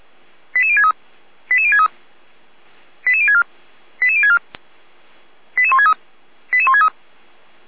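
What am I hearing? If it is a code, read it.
Five-tone selective call (ZVEI), heard as quick runs of five tones, tones of about 70 ms. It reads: 88063 88063 88064 88064 88163 88163
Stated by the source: ZVEI five-tone selective call received on an SDR.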